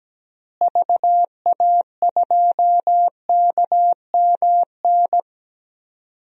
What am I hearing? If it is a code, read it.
Morse code, heard as a keyed tone, about 17 wpm, 700 Hz. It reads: VA2KMN
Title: DXCC-100 Call Signs - 6 Characters - Repeated 3x - 17wpm